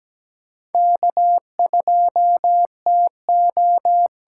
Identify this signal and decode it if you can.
Morse code, heard as a keyed tone, about 17 wpm, 700 Hz. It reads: K2TO
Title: USA Call Signs - 5 Characters - 8x Character Spacing - 17wpm